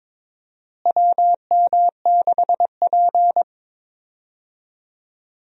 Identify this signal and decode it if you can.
Morse code, heard as a keyed tone, about 22 wpm, 700 Hz. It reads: WM6P
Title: USA Call Signs - 4 Characters - Repeated 3x - 22wpm